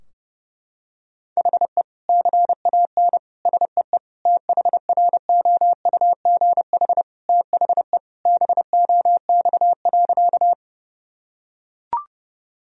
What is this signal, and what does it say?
Morse code, heard as a keyed tone, about 30 wpm, 700 Hz. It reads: HE CAN SEE THROUGH THE BOX.
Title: Sentences from Top 1000 Words - Review Part 2 - 30wpm